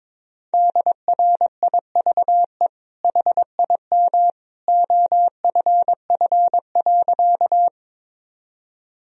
Morse code, heard as a keyed tone, about 22 wpm, 700 Hz. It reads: DRIVE HIM OFF.
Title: Sentences from Top 1000 Words - Review Part 2 - 22wpm